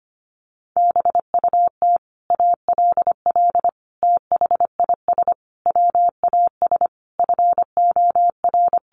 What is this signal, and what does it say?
Morse code, heard as a keyed tone, about 25 wpm, 700 Hz. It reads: BUT ALL THIS WAS FOR
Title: Sentences from Top 1400 Words 25wpm